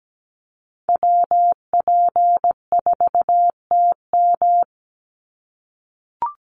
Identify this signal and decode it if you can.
Morse code, heard as a keyed tone, about 17 wpm, 700 Hz. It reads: WP4TM
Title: USA Call Signs 17wpm